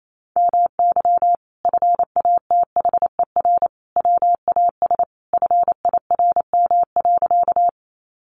Morse code, heard as a keyed tone, about 28 wpm, 700 Hz. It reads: MY FATHER WAS FIRM.
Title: Sentences from Top 1400 Words 28wpm